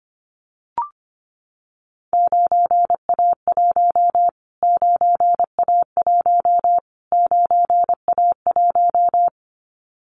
Morse code, heard as a keyed tone, about 25 wpm, 700 Hz. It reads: 9A1 9A1 9A1
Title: Callsign Prefixes for Top 25 DXCC - Repeated 3x - 25wpm